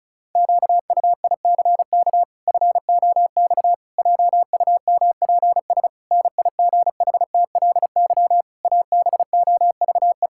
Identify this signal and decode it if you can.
Morse code, heard as a keyed tone, about 36 wpm, 700 Hz. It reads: QUICK FOX JUMPS NIGHTLY ABOVE
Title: Pangram Challenge v2 - 35wpm